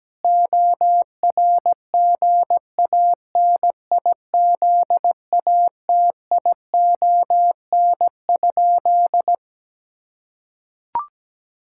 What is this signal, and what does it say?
Morse code, heard as a keyed tone, about 17 wpm, 700 Hz. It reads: ORGANIZATION?
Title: Sentences from Top 1000 Words - Review Part 5 - 17wpm